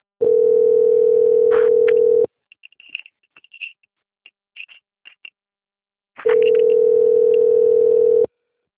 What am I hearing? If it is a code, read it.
Ringback tone (North America), 440 + 480 Hz.